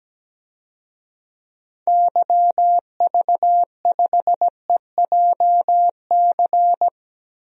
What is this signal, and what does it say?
Morse code, heard as a keyed tone, about 17 wpm, 700 Hz. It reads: YV5EJC
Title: DXCC-100 Call Signs - 6 Characters - 17wpm